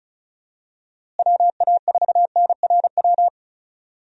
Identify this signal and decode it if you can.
Morse code, heard as a keyed tone, about 34 wpm, 700 Hz. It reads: WA4NRW
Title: USA Call Signs - 6 Characters - Repeated 3x - 35wpm